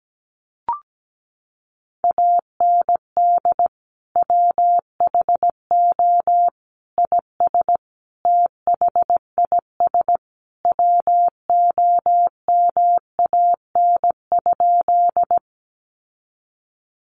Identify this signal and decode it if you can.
Morse code, heard as a keyed tone, about 17 wpm, 700 Hz. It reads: AND WHO IS THIS WOMAN?
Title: Sentences from Top 1100 Words 17wpm